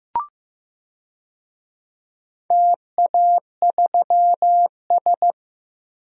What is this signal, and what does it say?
Morse code, heard as a keyed tone, about 16 wpm, 700 Hz. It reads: TA3S
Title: DXCC-100 Call Signs - 4 Characters - Repeated 3x - 15wpm